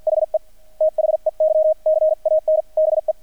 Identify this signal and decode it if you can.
Morse code, heard as a keyed tone, about 50 wpm, spaced as for 44 wpm, 630 Hz. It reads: HE THEY CAN BE
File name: archived / CRSnet